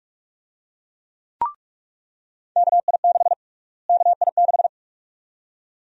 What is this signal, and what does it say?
Morse code, heard as a keyed tone, about 44 wpm, 700 Hz. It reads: KI6 KI6
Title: USA Call Sign Prefixes - Repeated 2x - 45wpm